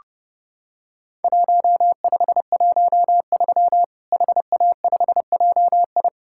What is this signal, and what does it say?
Morse code, heard as a keyed tone, about 30 wpm, 700 Hz. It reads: 1513 HA5JI
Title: Worked All Europe DX Contest - QTC Exchange - 30wpm